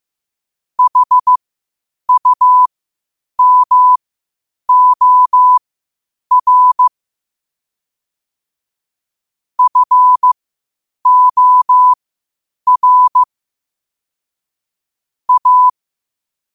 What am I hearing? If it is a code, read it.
Morse code, heard as a keyed tone, about 15 wpm, spaced as for 8 wpm, 1000 Hz. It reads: HUMOR FOR A
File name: qotdfiles / 8wpm